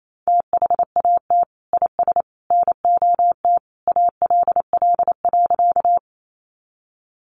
Morse code, heard as a keyed tone, about 28 wpm, 700 Hz. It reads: THAT IS NOT ALL.
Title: Sentences from Top 600 Words 28wpm